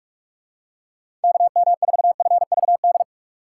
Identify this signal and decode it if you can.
Morse code, heard as a keyed tone, about 46 wpm, 700 Hz. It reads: KM4FVD
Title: USA Call Signs - 6 Characters - 4x Character Spacing - 45wpm